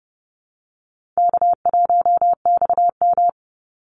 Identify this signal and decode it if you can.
Morse code, heard as a keyed tone, about 30 wpm, 700 Hz. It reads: K1XM
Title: Easy Call Signs Repeated 3x - 30wpm